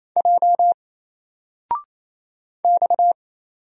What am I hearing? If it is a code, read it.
Morse code, heard as a keyed tone, about 28 wpm, 700 Hz. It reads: J X
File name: Single-Letter-Number-28wpm